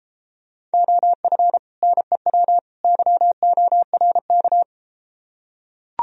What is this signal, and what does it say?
Morse code, heard as a keyed tone, about 34 wpm, 700 Hz. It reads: OF NEW YORK